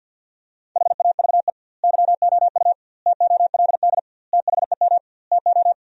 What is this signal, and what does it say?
Morse code, heard as a keyed tone, about 46 wpm, 700 Hz. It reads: HAVE YOU TOLD THEM TO
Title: 2x Word-Spacing - Sentences from Top 500 Words - 50wpm